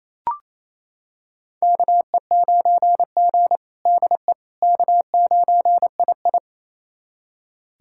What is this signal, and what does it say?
Morse code, heard as a keyed tone, about 28 wpm, 700 Hz. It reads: KE9G DE K9II
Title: USA Callsign DE Callsign - 4 Characters - 28wpm